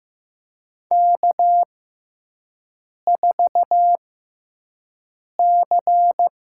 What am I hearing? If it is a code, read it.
Morse code, heard as a keyed tone, about 15 wpm, spaced as for 5 wpm, 700 Hz. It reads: K4C
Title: USA Call Signs - 4 Characters - 6x Character Spacing - 15wpm